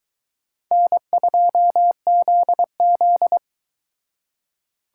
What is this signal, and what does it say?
Morse code, heard as a keyed tone, about 22 wpm, 700 Hz. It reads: N2ZZ